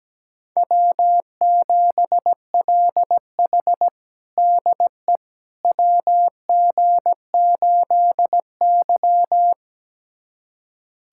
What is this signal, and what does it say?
Morse code, heard as a keyed tone, about 17 wpm, 700 Hz. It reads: W7LH DE WG8Y
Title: USA Callsign DE Callsign - 4 Characters - 17wpm